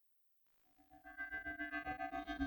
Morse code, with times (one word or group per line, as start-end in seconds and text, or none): none